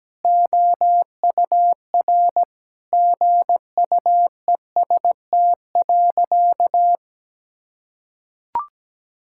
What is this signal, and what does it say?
Morse code, heard as a keyed tone, about 17 wpm, 700 Hz. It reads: OUR GUEST.
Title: Sentences from Top 1600 Words 17wpm